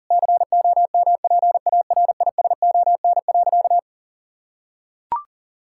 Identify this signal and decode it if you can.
Morse code, heard as a keyed tone, about 37 wpm, 700 Hz. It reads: COMPARISON.
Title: Sentences from Top 1600 Words 40wpm